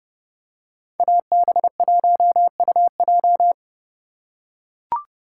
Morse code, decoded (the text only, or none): AB1UJ